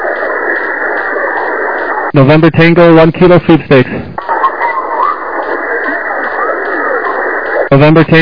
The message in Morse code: none